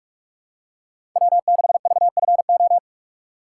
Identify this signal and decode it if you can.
Morse code, heard as a keyed tone, about 46 wpm, 700 Hz. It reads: W6VFO
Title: USA Call Signs - 5 Characters - 4x Character Spacing - 45wpm